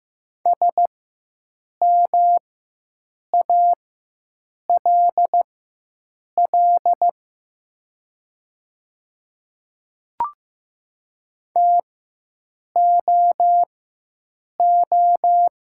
Morse code, heard as a keyed tone, about 15 wpm, spaced as for 7 wpm, 700 Hz. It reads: SMALL TOO
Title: 4x Character Spacing - Top 200 Words - 15wpm